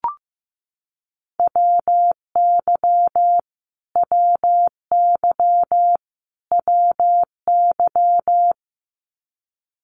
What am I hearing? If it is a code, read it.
Morse code, heard as a keyed tone, about 15 wpm, 700 Hz. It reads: WY WY WY
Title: US States Repeated 3x 15wpm